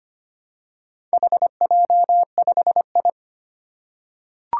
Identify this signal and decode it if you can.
Morse code, heard as a keyed tone, about 25 wpm, 700 Hz. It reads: HJ5I
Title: Four Challenging Letter-Number Combinations 25wpm